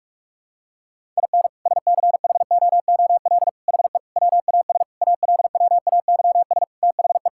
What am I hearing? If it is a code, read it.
Morse code, heard as a keyed tone, about 41 wpm, 700 Hz. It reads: IN SCHOOL HE WAS ALWAYS THE
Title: Sentences from Top 1000 Words - Review Part 5 - 45wpm